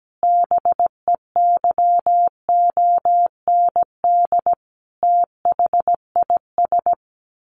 Morse code, heard as a keyed tone, about 17 wpm, 700 Hz. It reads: BEYOND THIS
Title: A Sound Of Thunder 17wpm